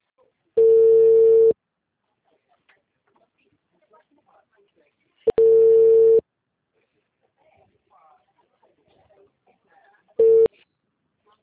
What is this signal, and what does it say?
Ringback tone (China, 450 Hz).
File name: out- -2018- - - .13465